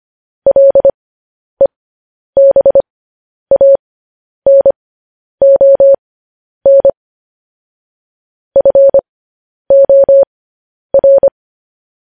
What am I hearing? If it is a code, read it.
Morse code, heard as a keyed tone, about 25 wpm, spaced as for 10 wpm, 550 Hz. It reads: LEBANON FOR